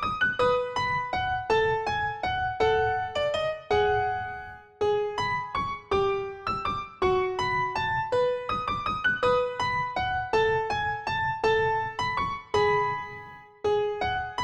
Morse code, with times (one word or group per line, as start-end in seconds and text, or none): none